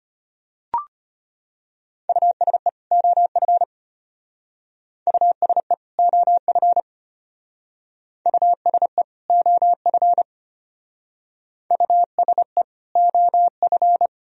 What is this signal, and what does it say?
Morse code, heard as a keyed tone, about 32 wpm, 700 Hz. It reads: USE OF USE OF USE OF USE OF